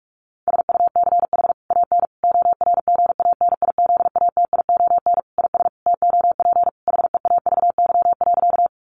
Morse code, heard as a keyed tone, about 44 wpm, 700 Hz. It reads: SUCH AN ORGANIZATION IS TOP HEAVY.
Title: Sentences from Top 1100 Words 45wpm